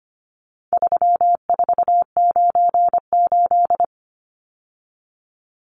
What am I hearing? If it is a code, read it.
Morse code, heard as a keyed tone, about 25 wpm, 700 Hz. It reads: 3498
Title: Numbers - 4-Digit 25wpm